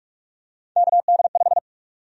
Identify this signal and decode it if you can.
Morse code, heard as a keyed tone, about 43 wpm, 700 Hz. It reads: KD5
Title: USA Call Sign Prefixes - 45wpm